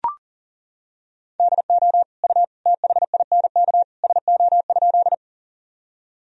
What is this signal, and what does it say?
Morse code, heard as a keyed tone, about 40 wpm, 700 Hz. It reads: DO U THINK SO?
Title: Sentences from Top 200 Words and Abbreviations 40wpm